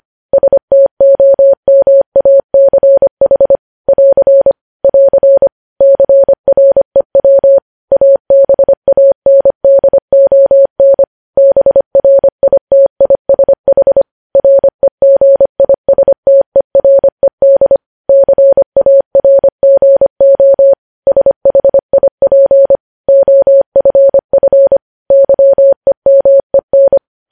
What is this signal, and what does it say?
Morse code, heard as a keyed tone, about 25 wpm, 550 Hz. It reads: STOMACH <AR> <AR> CREW ABANDON BRITISH REGISTERED CARGO SHIP OFF YEMEN
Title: News Headlines at 25